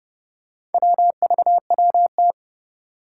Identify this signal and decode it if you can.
Morse code, heard as a keyed tone, about 30 wpm, 700 Hz. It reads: WVWT
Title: Four Challenging Letter-Number Combinations 30wpm